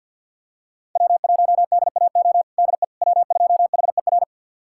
Morse code, heard as a keyed tone, about 48 wpm, 700 Hz. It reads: W1DAO DE W2HER
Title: USA Callsign DE Callsign - 5 Characters - 50wpm